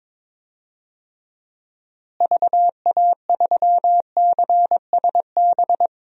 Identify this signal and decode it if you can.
Morse code, heard as a keyed tone, about 21 wpm, 700 Hz. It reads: VA3CSB
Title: DXCC-100 Call Signs - 6 Characters - Repeated 3x - 22wpm